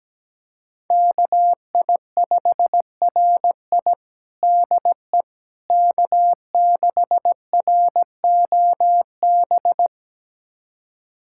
Morse code, heard as a keyed tone, about 17 wpm, 700 Hz. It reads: KI5RI DE K6ROB
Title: USA Callsign DE Callsign - 5 Characters - 17wpm